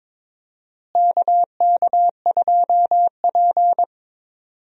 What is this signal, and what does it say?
Morse code, heard as a keyed tone, about 22 wpm, 700 Hz. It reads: KK2P
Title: USA Call Signs - 4 Characters - Repeated 3x - 22wpm